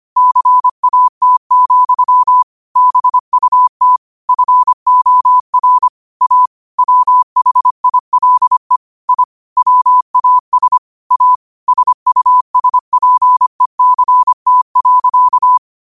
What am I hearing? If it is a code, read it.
Morse code, heard as a keyed tone, about 25 wpm, 1000 Hz. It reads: CAT, BUT FOR A WHILE I WAS A SUSPECT.
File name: qotdfiles / 25wpm